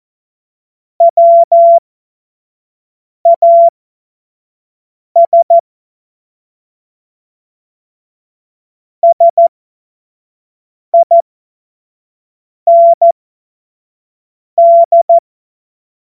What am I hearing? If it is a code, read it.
Morse code, heard as a keyed tone, about 14 wpm, spaced as for 5 wpm, 680 Hz. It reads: WAS SIND
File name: Morsekurs / Witze / Witz 24 Juni 2025 25 F70